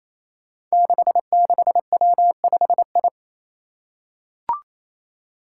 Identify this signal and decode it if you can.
Morse code, heard as a keyed tone, about 28 wpm, 700 Hz. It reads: 66W5I
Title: Five Challenging Letter-Number Combinations 28wpm